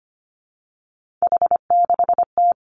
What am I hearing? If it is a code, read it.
Morse code, heard as a keyed tone, about 25 wpm, 700 Hz. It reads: H6T
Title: Three Challenging Letter-Number Combinations 25wpm